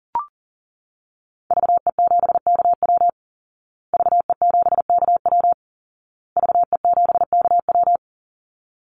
Morse code, heard as a keyed tone, about 40 wpm, 700 Hz. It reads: VE7KW VE7KW VE7KW
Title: Call Signs Repeated 2x - 40wpm